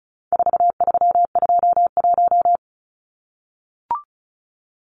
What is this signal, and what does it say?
Morse code, heard as a keyed tone, about 34 wpm, 700 Hz. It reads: 4321
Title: Numbers - 4-Digit 35wpm